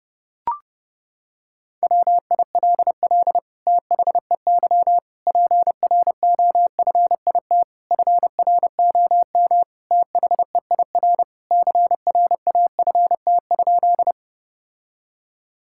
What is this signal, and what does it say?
Morse code, heard as a keyed tone, about 32 wpm, 700 Hz. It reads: WILL THEY PROFIT FROM THEIR CRAFT?
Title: Sentences from Top 1000 Words - Review Part 6 - 30wpm